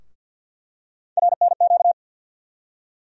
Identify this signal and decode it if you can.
Morse code, heard as a keyed tone, about 50 wpm, 700 Hz. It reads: RNQ
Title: USA Call Sign Postfixes - 3 Letters 50wpm